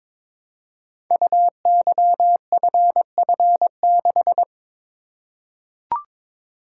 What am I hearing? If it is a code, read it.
Morse code, heard as a keyed tone, about 22 wpm, 700 Hz. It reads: UYFF6
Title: Code Groups - 5 Letter:Numbers - 22wpm